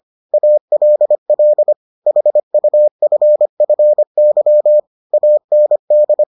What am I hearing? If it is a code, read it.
Morse code, heard as a keyed tone, about 25 wpm, 600 Hz. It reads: ALL HUFFY AND